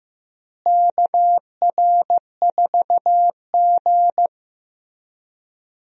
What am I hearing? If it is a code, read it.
Morse code, heard as a keyed tone, about 15 wpm, 700 Hz. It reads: KR4G
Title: USA Call Signs - 4 Characters - Repeated 3x - 15wpm